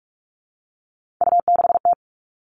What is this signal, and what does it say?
Morse code, heard as a keyed tone, about 46 wpm, 700 Hz. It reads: U6T
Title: Three Challenging Letter-Number Combinations 45wpm